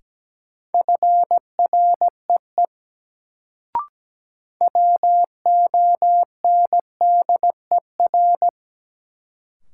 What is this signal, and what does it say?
Morse code, heard as a keyed tone, about 17 wpm, 700 Hz. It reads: FREE WONDER
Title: Sentences from Top 500 Words - New Word Drill - 17wpm